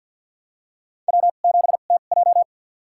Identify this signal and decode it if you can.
Morse code, heard as a keyed tone, about 47 wpm, spaced as for 35 wpm, 700 Hz. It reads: W7TJ
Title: USA Call Signs - 4 Characters - 2x Character Spacing - 50wpm